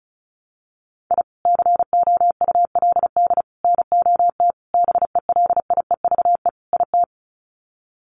Morse code, heard as a keyed tone, about 34 wpm, 700 Hz. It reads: I COULD NOT BELIEVE IT
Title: Sets of 5 Words Encore 35wpm